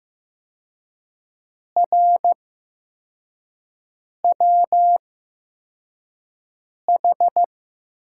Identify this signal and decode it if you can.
Morse code, heard as a keyed tone, about 15 wpm, spaced as for 4 wpm, 700 Hz. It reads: RWH
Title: USA Call Signs - 5 Characters - 8x Character Spacing - 15wpm